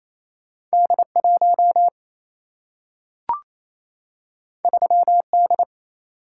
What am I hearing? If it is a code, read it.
Morse code, heard as a keyed tone, about 28 wpm, 700 Hz. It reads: D 1 3 D
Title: Single-Letter Single-Number 28wpm